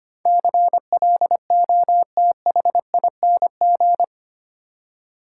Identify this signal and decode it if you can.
Morse code, heard as a keyed tone, about 25 wpm, 700 Hz. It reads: CLOTHING
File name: 8 Letter Words - First 3 Letters Given - 25wpm